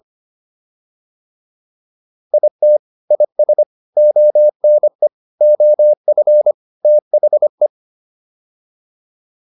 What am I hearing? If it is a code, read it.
Morse code, heard as a keyed tone, about 25 wpm, 600 Hz. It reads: IT IS ONE OF THE